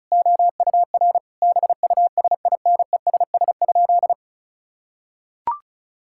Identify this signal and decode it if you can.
Morse code, heard as a keyed tone, about 36 wpm, 700 Hz. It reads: OUR BUSINESS?
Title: Sentences from Top 1000 Words - Review Part 5 - 35wpm